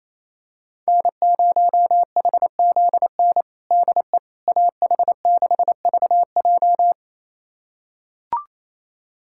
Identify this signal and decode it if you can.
Morse code, heard as a keyed tone, about 27 wpm, 700 Hz. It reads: N0HZN DE AH6VJ
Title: USA Callsign DE Callsign - 5 Characters - 28wpm